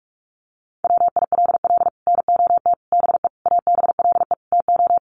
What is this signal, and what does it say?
Morse code, heard as a keyed tone, about 44 wpm, 700 Hz. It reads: WILL NOT BE ABLE TO
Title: Sets of 5 Words 45wpm